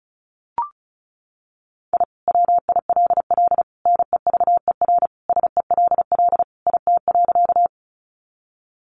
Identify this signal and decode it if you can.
Morse code, heard as a keyed tone, about 34 wpm, 700 Hz. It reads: I WILL NEVER SELL IT.
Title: Sentences from Top 1100 Words 35wpm